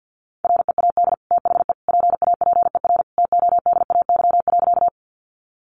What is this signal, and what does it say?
Morse code, heard as a keyed tone, about 50 wpm, 700 Hz. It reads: READ THE PAPER TODAY.
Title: Sentences from Top 600 Words 50wpm